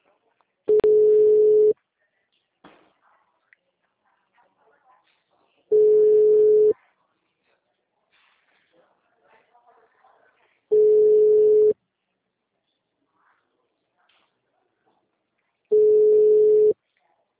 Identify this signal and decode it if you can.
Ringback tone (Europe, 425 Hz), with speech.